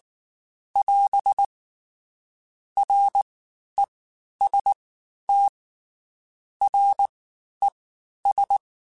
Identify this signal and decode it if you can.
Morse code, heard as a keyed tone, about 19 wpm, spaced as for 11 wpm, 800 Hz. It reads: <AS> REST RES
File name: Basic Sets / Basic Letter Groups Collective7 / JZ17@18